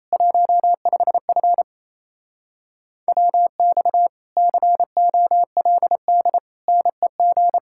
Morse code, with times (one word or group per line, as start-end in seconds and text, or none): 0.1-1.6 15F
3.1-4.1 WX
4.4-6.4 COLD
6.7-7.6 NEG